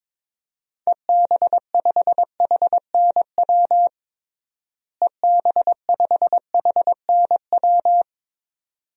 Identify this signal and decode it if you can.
Morse code, heard as a keyed tone, about 22 wpm, 700 Hz. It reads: EB5HNW EB5HNW
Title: DXCC-100 Call Signs - 6 Characters - Repeated 2x - 22wpm